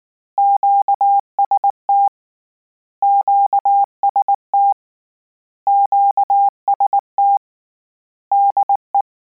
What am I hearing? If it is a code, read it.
Morse code, heard as a keyed tone, about 19 wpm, 800 Hz. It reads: QST QST QST DE
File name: Combinations / Combinations-mp3s / Combinations@18